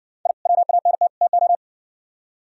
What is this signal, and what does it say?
Morse code, heard as a keyed tone, about 58 wpm, 700 Hz. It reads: I WANT TO